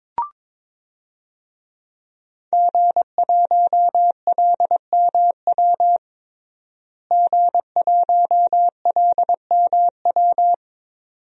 Morse code, heard as a keyed tone, about 22 wpm, 700 Hz. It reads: G1LMW G1LMW
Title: DXCC-100 Call Signs - 5 Characters - Repeated 2x - 22wpm